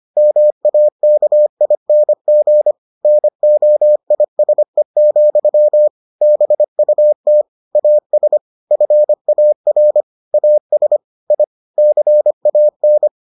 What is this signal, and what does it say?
Morse code, heard as a keyed tone, about 25 wpm, 600 Hz. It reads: MAKING NOISE, BUT AS FAR AS I CAN